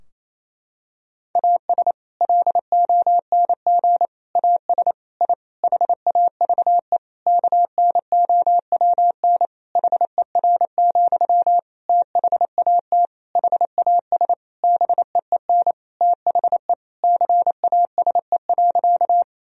Morse code, heard as a keyed tone, about 28 wpm, 700 Hz. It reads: AS LONG AS I HAVE KNOWN HER, THAT HAS BEEN THE CASE.
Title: Sentences from Top 1000 Words - Review Part 3 - 28wpm